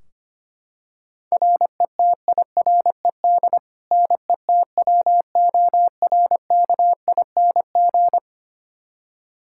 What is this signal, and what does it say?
Morse code, heard as a keyed tone, about 25 wpm, 700 Hz. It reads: RETIRED NETWORKING
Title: K1USN SST QSO Practice 25wpm